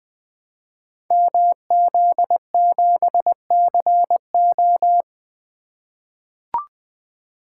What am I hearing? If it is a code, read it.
Morse code, heard as a keyed tone, about 20 wpm, 700 Hz. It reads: MZ7CO
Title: Code Groups - 5 Letter:Numbers - 20wpm